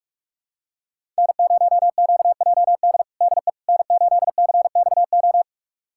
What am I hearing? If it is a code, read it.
Morse code, heard as a keyed tone, about 46 wpm, 700 Hz. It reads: N0QJD DE N8CXO
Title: USA Callsign DE Callsign - 5 Characters - 45wpm